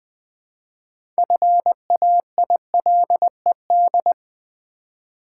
Morse code, heard as a keyed tone, about 20 wpm, 700 Hz. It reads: FAILED
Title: 6 Letter Words - First 3 Letters Given - 20wpm